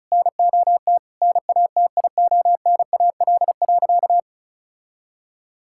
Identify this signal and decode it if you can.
Morse code, heard as a keyed tone, about 36 wpm, 700 Hz. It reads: NOT NATIONAL.
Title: Sentences from Top 1000 Words - Review Part 5 - 35wpm